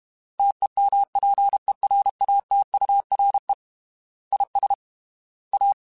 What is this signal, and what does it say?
Morse code, heard as a keyed tone, about 32 wpm, 800 Hz. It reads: TEMPERATURE IS A
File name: Weather Sets / Weather Sets-mp3s / Weather 4 14 @30